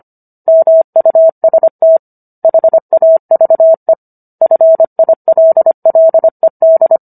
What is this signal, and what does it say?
Morse code, heard as a keyed tone, about 25 wpm, 650 Hz. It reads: MUST HAVE FILLED